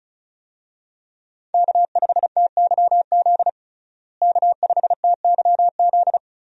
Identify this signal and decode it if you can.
Morse code, heard as a keyed tone, about 36 wpm, 700 Hz. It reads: K5TYZ K5TYZ
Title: USA Call Signs - 5 Characters - Repeated 3x - 35wpm